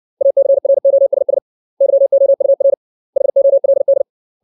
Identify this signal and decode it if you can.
Morse code, heard as a keyed tone, about 60 wpm, 550 Hz. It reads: ACROSS YOUR HOLD
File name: 632 words / 632 Words 60WPM